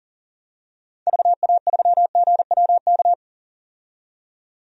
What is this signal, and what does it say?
Morse code, heard as a keyed tone, about 40 wpm, 700 Hz. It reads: VA3GWK
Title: DXCC-100 Call Signs - 6 Characters - 40wpm